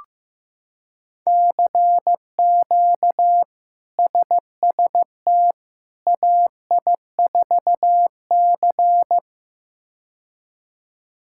Morse code, heard as a keyed tone, about 15 wpm, 700 Hz. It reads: CQ SST AI4C
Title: K1USN SST QSO Practice - 15wpm